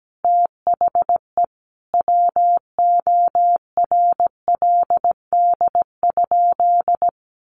Morse code, heard as a keyed tone, about 17 wpm, 700 Hz. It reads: THE WORLD?
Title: Sentences from Top 200 Words 17wpm